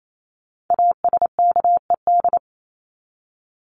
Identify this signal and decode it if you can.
Morse code, heard as a keyed tone, about 28 wpm, 700 Hz. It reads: ASKED
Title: ED Suffix - Top 100 Words with Base Words 28wpm